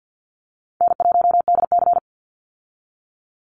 Morse code, heard as a keyed tone, about 50 wpm, 700 Hz. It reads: N1DC
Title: Easy Call Signs - 50wpm